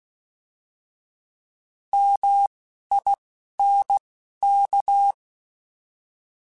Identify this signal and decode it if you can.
Morse code, heard as a keyed tone, about 16 wpm, spaced as for 11 wpm, 800 Hz. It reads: MINK
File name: Alphabetical / Alphabetical mp3s / Ms @10